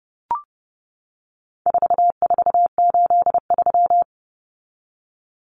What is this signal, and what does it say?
Morse code, heard as a keyed tone, about 29 wpm, 700 Hz. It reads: 4483
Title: Common Cut Numbers - 4-Digit 30wpm